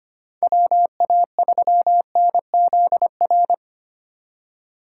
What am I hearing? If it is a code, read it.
Morse code, heard as a keyed tone, about 25 wpm, 700 Hz. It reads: WA3NZR
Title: K1USN SST QSO Practice - 25wpm